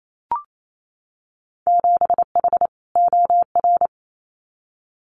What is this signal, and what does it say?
Morse code, heard as a keyed tone, about 28 wpm, 700 Hz. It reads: 7H OR
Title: Winter Field Day Exchanges Encore 28wpm